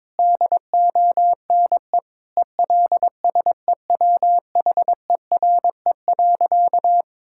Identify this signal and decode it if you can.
Morse code, heard as a keyed tone, about 22 wpm, 700 Hz. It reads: DONE ELSEWHERE.
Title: Sentences from Top 1600 Words 22wpm